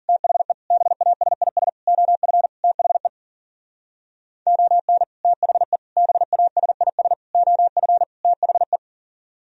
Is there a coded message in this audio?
Yes: Morse code from 0.1 to 8.8 s, heard as a keyed tone, about 52 wpm, 700 Hz. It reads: THE BASIS OF THE ON THE BASIS OF THE